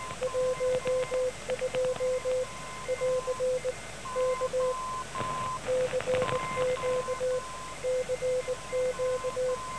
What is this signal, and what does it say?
Morse code, heard as a keyed tone, about 19 wpm, 510 Hz. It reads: 12 <AR> K CQ CQ
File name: TAH 17008.5 0900